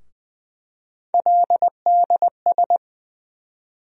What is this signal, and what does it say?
Morse code, heard as a keyed tone, about 20 wpm, 700 Hz. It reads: LDS
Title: USA Call Sign Postfixes - 3 Letters 20wpm